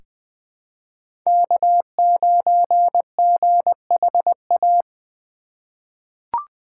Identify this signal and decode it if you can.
Morse code, heard as a keyed tone, about 20 wpm, 700 Hz. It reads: K9GHA
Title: USA Call Signs - 5 Characters - 3x Character Spacing - 20wpm